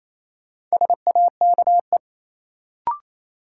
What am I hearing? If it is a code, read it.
Morse code, heard as a keyed tone, about 28 wpm, 700 Hz. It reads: SAKE
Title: Sets of 4 Words - New Word Drill - 28wpm